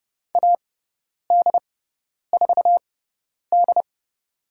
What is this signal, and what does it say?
Morse code, heard as a keyed tone, about 30 wpm, spaced as for 10 wpm, 700 Hz. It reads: AD4D
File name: Lesson 17g - Letter u - Call Signs 30wpm